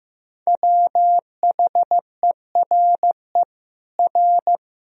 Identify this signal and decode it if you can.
Morse code, heard as a keyed tone, about 16 wpm, 700 Hz. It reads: WHERE R
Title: Sentences from Top 200 Words and Abbreviations 15wpm